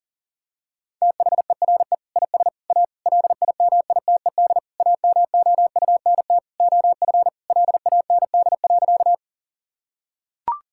Morse code, heard as a keyed tone, about 40 wpm, 700 Hz. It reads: THERE IS A LIMITED AMOUNT OF LAND.
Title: Sentences from Top 1000 Words - Review Part 6 - 40wpm